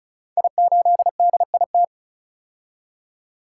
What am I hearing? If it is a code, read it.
Morse code, heard as a keyed tone, about 36 wpm, 700 Hz. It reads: I8DIT
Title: Code Groups - 5 Letter:Numbers - 35wpm